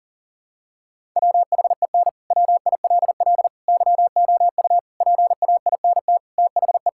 Morse code, heard as a keyed tone, about 40 wpm, 700 Hz. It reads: WHEN WILL YOU PAINT THE
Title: Sentences from Top 1000 Words - Review Part 6 - 40wpm